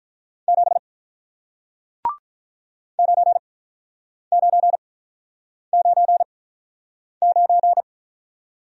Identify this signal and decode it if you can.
Morse code, heard as a keyed tone, about 48 wpm, spaced as for 9 wpm, 700 Hz. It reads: 7 9999